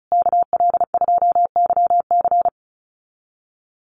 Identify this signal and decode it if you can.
Morse code, heard as a keyed tone, about 34 wpm, 700 Hz. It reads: KL2YC
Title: USA Call Signs Repeated 3x 35wpm